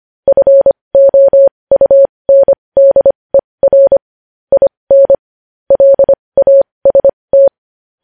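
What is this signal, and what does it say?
Morse code, heard as a keyed tone, about 25 wpm, spaced as for 20 wpm, 550 Hz. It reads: FOUNDER IN LAST
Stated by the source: Morse code practice of news headlines at 20 wpm.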